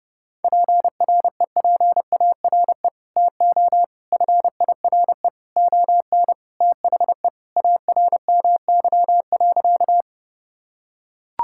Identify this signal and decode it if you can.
Morse code, heard as a keyed tone, about 32 wpm, 700 Hz. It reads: PREPARE TO FIRE ON THE ARMY.
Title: Sentences from Top 1000 Words - Review Part 6 - 30wpm